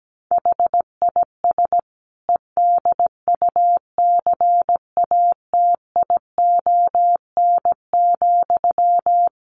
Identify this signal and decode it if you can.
Morse code, heard as a keyed tone, about 17 wpm, 700 Hz. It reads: HIS EDUCATION,
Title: Sentences from Top 1400 Words 17wpm